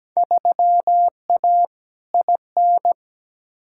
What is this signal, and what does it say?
Morse code, heard as a keyed tone, about 17 wpm, 700 Hz. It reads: 3A IN
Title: Field Day Exchanges - 17wpm